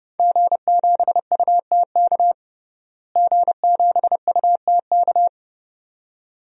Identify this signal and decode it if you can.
Morse code, heard as a keyed tone, about 30 wpm, 700 Hz. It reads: G7UTK G7UTK
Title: DXCC-100 Call Signs - 5 Characters - Repeated 2x - 30wpm